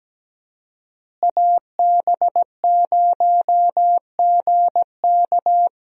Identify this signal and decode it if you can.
Morse code, heard as a keyed tone, about 17 wpm, 700 Hz. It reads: AB0GK
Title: USA Call Signs - 5 Characters - Repeated 2x - 17wpm